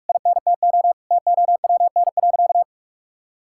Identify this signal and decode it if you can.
Morse code, heard as a keyed tone, about 46 wpm, 700 Hz. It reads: INTO TOWN.
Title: Sentences from Top 1000 Words - Review Part 6 - 45wpm